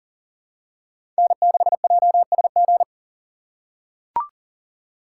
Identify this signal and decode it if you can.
Morse code, heard as a keyed tone, about 40 wpm, 700 Hz. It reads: N6JSG
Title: USA Call Signs - 5 Characters - 4x Character Spacing - 40wpm